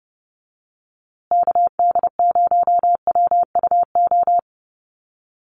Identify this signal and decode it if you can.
Morse code, heard as a keyed tone, about 30 wpm, 700 Hz. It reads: KD0WUO
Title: USA Call Signs Repeated 3x 30wpm